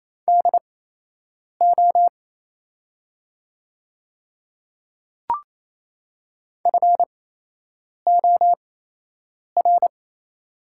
Morse code, heard as a keyed tone, about 28 wpm, spaced as for 8 wpm, 700 Hz. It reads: DO FOR
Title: 8x Character Spacing - Top 100 Words - 28wpm